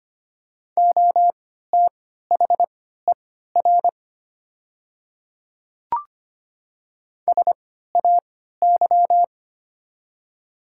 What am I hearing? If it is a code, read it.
Morse code, heard as a keyed tone, about 25 wpm, spaced as for 14 wpm, 700 Hz. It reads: OTHER SAY
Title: 3x Character Spacing - Top 300 Words - 25wpm